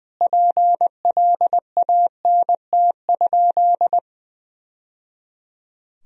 Morse code, heard as a keyed tone, about 20 wpm, 700 Hz. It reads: PLANT?
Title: Sentences from Top 1000 Words - Review Part 1 - 20wpm